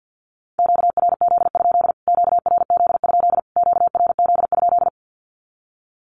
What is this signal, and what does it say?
Morse code, heard as a keyed tone, about 50 wpm, 700 Hz. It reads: QRZ? QRZ? QRZ?
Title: Q-Codes Repeated 3x 50wpm